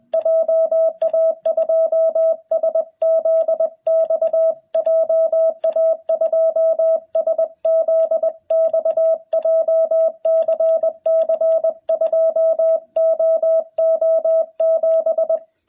Morse code, heard as a keyed tone, about 20 wpm, 650 Hz, with radio noise. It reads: JA2SZX JA2SZX JCC2OO7
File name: jh8cbh / 2021cw